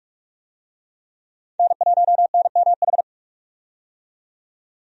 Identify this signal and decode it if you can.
Morse code, heard as a keyed tone, about 43 wpm, 700 Hz. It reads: N1NMH
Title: USA Call Signs - 5 Characters - 45wpm